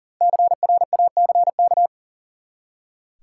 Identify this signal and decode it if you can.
Morse code, heard as a keyed tone, about 40 wpm, 700 Hz. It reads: CRACK
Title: Five Letter Words - 40wpm